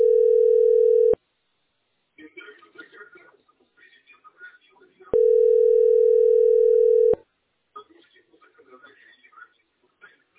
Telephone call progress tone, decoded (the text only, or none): ringback tone